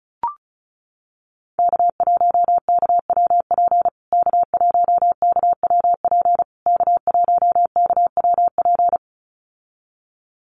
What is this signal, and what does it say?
Morse code, heard as a keyed tone, about 34 wpm, 700 Hz. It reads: K1KWP K1KWP K1KWP
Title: USA Call Signs Repeated 3x 35wpm